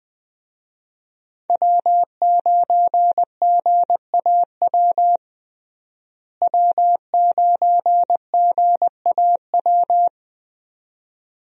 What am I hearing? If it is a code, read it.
Morse code, heard as a keyed tone, about 21 wpm, 700 Hz. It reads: W9GAW W9GAW
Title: USA Call Signs - 5 Characters - Repeated 2x - 20wpm